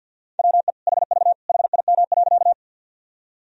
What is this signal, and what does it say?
Morse code, heard as a keyed tone, about 52 wpm, 700 Hz. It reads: WE HV HIM.
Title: Sentences from Top 100 Words and Abbreviations 50wpm